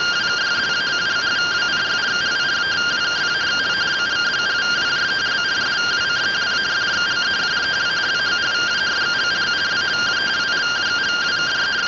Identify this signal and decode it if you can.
Two-tone teleprinter signal, about 170 Hz shift.
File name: findings / CT AXIAL -256.gen-03 rtty rx